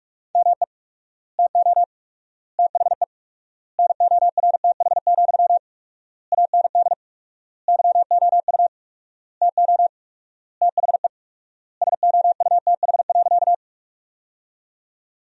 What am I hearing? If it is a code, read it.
Morse code, heard as a keyed tone, about 46 wpm, 700 Hz. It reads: ME TO THE NORTH, AND YOU TO THE SOUTH.
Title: 4x Word-Spacing - Sentences from Top 400 Words - 45wpm